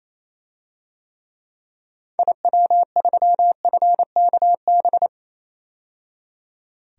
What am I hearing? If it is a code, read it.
Morse code, heard as a keyed tone, about 28 wpm, 700 Hz. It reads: IW3FKB
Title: DXCC-100 Call Signs - 6 Characters - 28wpm